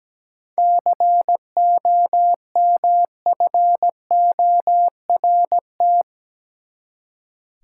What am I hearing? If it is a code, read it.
Morse code, heard as a keyed tone, about 17 wpm, 700 Hz. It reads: COMFORT
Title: Seven Letter Words - 17wpm